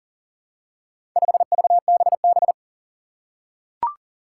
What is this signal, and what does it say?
Morse code, heard as a keyed tone, about 40 wpm, 700 Hz. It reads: 5VBB
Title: Four Challenging Letter-Number Combinations 40wpm